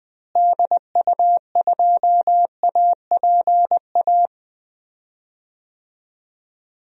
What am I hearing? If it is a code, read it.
Morse code, heard as a keyed tone, about 20 wpm, 700 Hz. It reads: DU2APA
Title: DXCC-100 Call Signs - 6 Characters - Repeated 3x - 20wpm